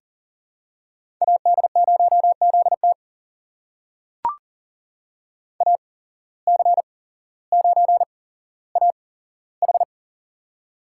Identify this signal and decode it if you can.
Morse code, heard as a keyed tone, about 39 wpm, spaced as for 17 wpm, 700 Hz. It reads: AD0ZT AC9AH
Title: USA Call Signs - 5 Characters - 8x Character Spacing - 40wpm